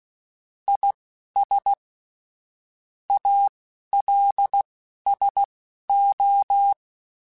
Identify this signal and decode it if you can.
Morse code, heard as a keyed tone, about 16 wpm, spaced as for 11 wpm, 800 Hz. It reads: IS ALSO
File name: Technical Subjects / Technical Subjects-mp3s / XC @10